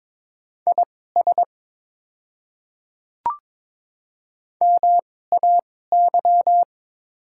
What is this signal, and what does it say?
Morse code, heard as a keyed tone, about 22 wpm, spaced as for 16 wpm, 690 Hz. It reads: IS MAY
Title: 2x Character Spacing - Top 100 Words - 22wpm